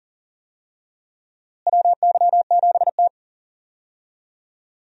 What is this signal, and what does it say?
Morse code, heard as a keyed tone, about 40 wpm, 700 Hz. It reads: WY7T
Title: USA Call Signs - 4 Characters - Repeated 2x - 40wpm